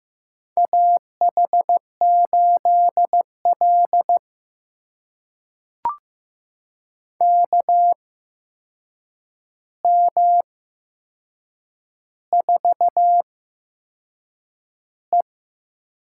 Morse code, heard as a keyed tone, about 15 wpm, spaced as for 6 wpm, 700 Hz. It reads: AH8L KM4E
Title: USA Call Signs - 8x Character Spacing - 15wpm